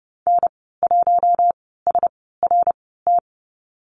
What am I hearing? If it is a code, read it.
Morse code, heard as a keyed tone, about 30 wpm, spaced as for 17 wpm, 700 Hz. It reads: N1SRT
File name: Lesson 14g - Number 2 - Call Signs 30wpm